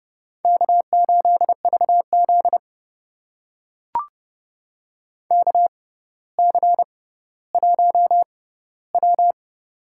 Morse code, heard as a keyed tone, about 30 wpm, spaced as for 15 wpm, 700 Hz. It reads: K8VZ KC1W
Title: USA Call Signs - 4 Characters - 6x Character Spacing - 30wpm